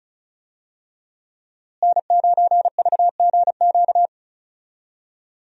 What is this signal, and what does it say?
Morse code, heard as a keyed tone, about 34 wpm, 700 Hz. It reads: N9VGQ
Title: USA Call Signs - 5 Characters - Repeated 2x - 35wpm